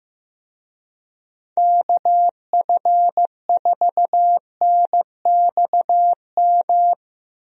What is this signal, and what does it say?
Morse code, heard as a keyed tone, about 15 wpm, 700 Hz. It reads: KF4NXM
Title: USA Call Signs - 6 Characters - Repeated 2x - 15wpm